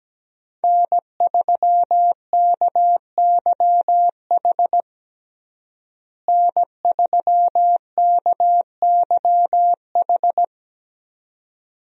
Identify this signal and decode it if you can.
Morse code, heard as a keyed tone, about 17 wpm, 700 Hz. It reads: N3KYH N3KYH
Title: USA Call Signs - 5 Characters - Repeated 2x - 17wpm